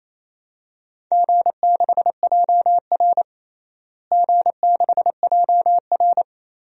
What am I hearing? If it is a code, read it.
Morse code, heard as a keyed tone, about 28 wpm, 700 Hz. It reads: G6JR G6JR
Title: DXCC-100 Call Signs - 4 Characters - Repeated 3x - 28wpm